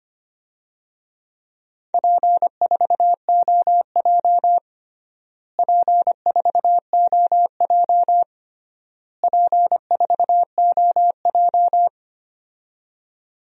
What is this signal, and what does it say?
Morse code, heard as a keyed tone, about 25 wpm, 700 Hz. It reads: P4OJ P4OJ P4OJ
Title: DXCC-100 Call Signs - 4 Characters - Repeated 3x - 25wpm